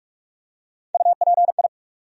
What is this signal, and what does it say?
Morse code, heard as a keyed tone, about 46 wpm, 700 Hz. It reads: UPI
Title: USA Call Sign Postfixes - 3 Letters 45wpm